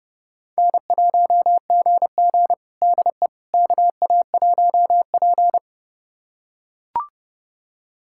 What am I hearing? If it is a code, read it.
Morse code, heard as a keyed tone, about 30 wpm, 700 Hz. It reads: N1GG DE KA1P
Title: USA Callsign DE Callsign - 4 Characters - 30wpm